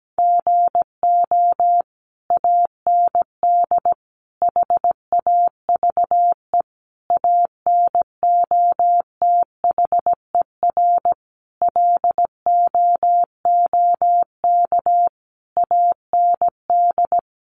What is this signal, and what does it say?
Morse code, heard as a keyed tone, about 17 wpm, 700 Hz. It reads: GO AND HAVE ANOTHER LOOK AND
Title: Sentences from Top 600 Words 17wpm